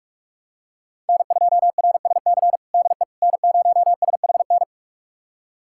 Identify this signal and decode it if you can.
Morse code, heard as a keyed tone, about 46 wpm, 700 Hz. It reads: N2RSC DE N0SHN
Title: USA Callsign DE Callsign - 5 Characters - 45wpm